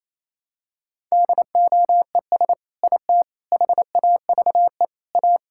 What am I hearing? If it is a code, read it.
Morse code, heard as a keyed tone, about 28 wpm, 700 Hz. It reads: DOES IT HAVE A
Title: Sentences from Top 1000 Words - Review Part 5 - 28wpm